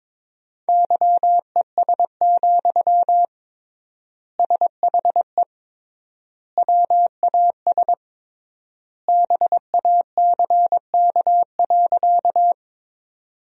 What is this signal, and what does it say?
Morse code, heard as a keyed tone, about 22 wpm, 700 Hz. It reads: YES, SHE WAS BACK.
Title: 3x Word-Spacing - Sentences from Top 500 Words - 22wpm